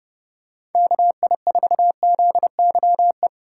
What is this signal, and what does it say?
Morse code, heard as a keyed tone, about 32 wpm, 700 Hz. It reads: KI4ZYE
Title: USA Call Signs - 6 Characters - 3x Character Spacing - 30wpm